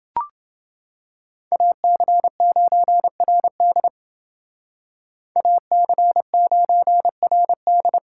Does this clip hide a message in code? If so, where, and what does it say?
Yes: Morse code from 1.5 to 8.0 s, heard as a keyed tone, about 28 wpm, 700 Hz. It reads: AC9RD AC9RD